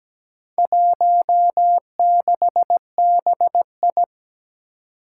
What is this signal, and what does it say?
Morse code, heard as a keyed tone, about 17 wpm, 700 Hz. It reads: 16BI
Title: Four Challenging Letter-Number Combinations 17wpm